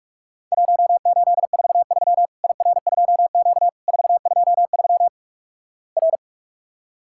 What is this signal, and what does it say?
Morse code, heard as a keyed tone, about 44 wpm, 700 Hz. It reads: 1843 IR2Q 423 R
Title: Worked All Europe DX Contest - QTC Exchange - 45wpm